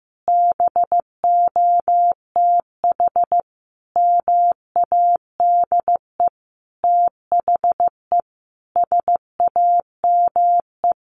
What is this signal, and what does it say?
Morse code, heard as a keyed tone, about 15 wpm, 700 Hz. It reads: BOTH MADE THE SAME
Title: Sentences from Top 1500 Words 15wpm